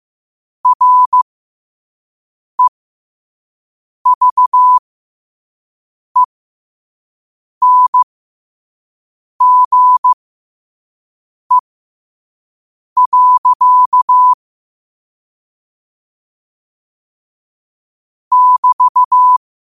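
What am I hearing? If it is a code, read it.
Morse code, heard as a keyed tone, about 15 wpm, spaced as for 5 wpm, 1000 Hz. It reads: REVENGE. <BT>
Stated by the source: Morse code sent at 5 wpm.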